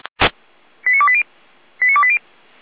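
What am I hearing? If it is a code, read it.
Five-tone selective call (ZVEI), heard as quick runs of five tones, tones of about 70 ms. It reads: 89280 89280
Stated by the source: ZVEI five-tone selective call received on an SDR.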